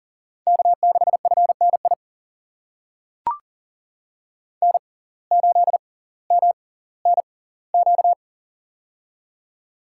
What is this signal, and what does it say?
Morse code, heard as a keyed tone, about 40 wpm, spaced as for 20 wpm, 700 Hz. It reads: K6FNI N8MNQ
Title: USA Call Signs - 5 Characters - 6x Character Spacing - 40wpm